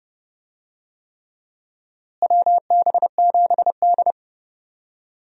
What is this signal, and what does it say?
Morse code, heard as a keyed tone, about 30 wpm, 700 Hz. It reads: WB7D
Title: USA Call Signs - 4 Characters - Repeated 3x - 30wpm